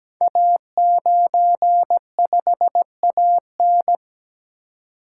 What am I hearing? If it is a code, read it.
Morse code, heard as a keyed tone, about 17 wpm, 700 Hz. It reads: A95AN
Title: Code Groups - 5 Letter:Numbers - 17wpm